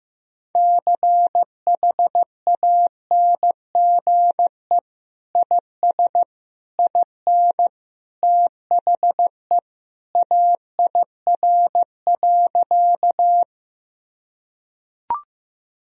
Morse code, heard as a keyed tone, about 15 wpm, 700 Hz. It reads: CHANGE IS IN THE AIR.
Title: Sentences from Top 1000 Words - Review Part 1 - 15wpm